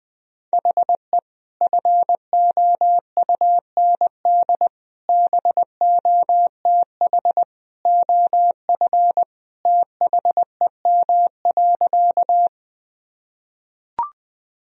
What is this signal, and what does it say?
Morse code, heard as a keyed tone, about 21 wpm, 700 Hz. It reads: HE FOUND BOTH OF THEM.
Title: Sentences from Top 1000 Words - Review Part 1 - 20wpm